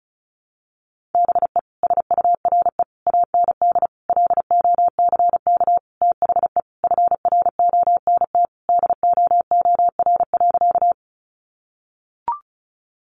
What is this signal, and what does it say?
Morse code, heard as a keyed tone, about 34 wpm, 700 Hz. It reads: BE SURE AND LOCK THE FRONT DOOR.
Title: Sentences from Top 1300 Words 35wpm